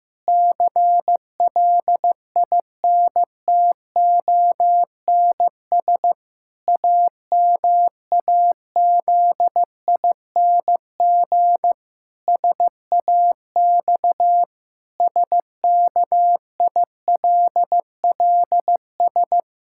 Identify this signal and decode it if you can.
Morse code, heard as a keyed tone, about 15 wpm, 700 Hz. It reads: CLINTONS AMAZING SAX SKILLS
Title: Pangram Challenge v2 - 15wpm